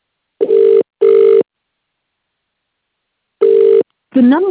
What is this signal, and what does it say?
Ringback tone (UK), 400 + 450 Hz, with speech.